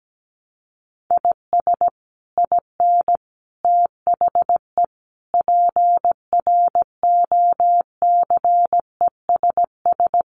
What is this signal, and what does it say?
Morse code, heard as a keyed tone, about 17 wpm, 700 Hz. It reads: IS IN THE PROCESS
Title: Sets of 5 Words Encore2 17wpm